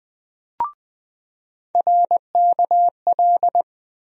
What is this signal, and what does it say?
Morse code, heard as a keyed tone, about 20 wpm, 700 Hz. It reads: RKL
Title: USA Call Sign Postfixes - 3 Letters 20wpm